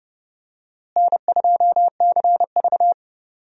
Morse code, heard as a keyed tone, about 30 wpm, 700 Hz. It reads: N2CV
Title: USA Call Signs - 4 Characters - Repeated 2x - 30wpm